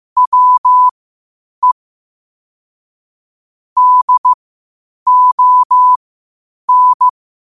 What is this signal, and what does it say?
Morse code, heard as a keyed tone, about 15 wpm, spaced as for 8 wpm, 1000 Hz. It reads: WE DON
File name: 8wpm